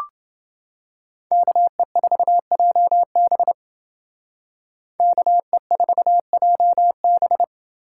Morse code, heard as a keyed tone, about 30 wpm, 700 Hz. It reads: KE4JB KE4JB